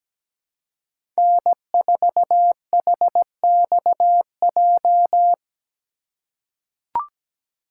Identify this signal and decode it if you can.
Morse code, heard as a keyed tone, about 17 wpm, 700 Hz. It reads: N4HXJ
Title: USA Call Signs - 5 Characters - 6x Character Spacing - 17wpm